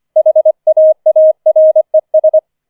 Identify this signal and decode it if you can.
Morse code, heard as a keyed tone, about 25 wpm, 630 Hz. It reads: HAARES
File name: 004 Vierter Streich 25wpm